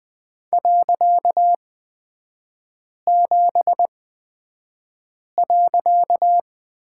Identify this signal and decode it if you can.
Morse code, heard as a keyed tone, about 20 wpm, spaced as for 5 wpm, 700 Hz. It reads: .7.
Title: 255 Lesson 41e - Period - Three Characters 20wpm